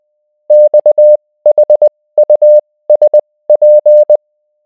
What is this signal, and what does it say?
Morse code, heard as a keyed tone, about 20 wpm, spaced as for 16 wpm, 600 Hz. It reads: XHUSP